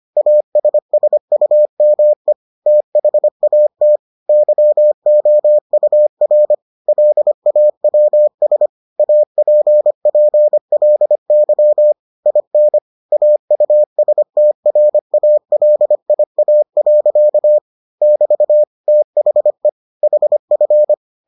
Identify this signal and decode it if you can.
Morse code, heard as a keyed tone, about 25 wpm, 600 Hz. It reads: ASSUME THAT YOUR LAWS APPLY IN AUSTRALIA. <BT> THE HF